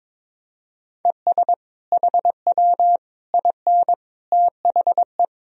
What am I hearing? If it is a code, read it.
Morse code, heard as a keyed tone, about 22 wpm, 700 Hz. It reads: ES HW IN THE
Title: Sentences from Top 200 Words and Abbreviations 22wpm